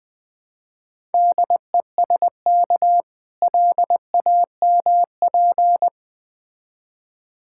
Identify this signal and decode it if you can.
Morse code, heard as a keyed tone, about 20 wpm, 700 Hz. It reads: DESK LAMP
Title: Easy Words - Speed-Racing with Hints - 20wpm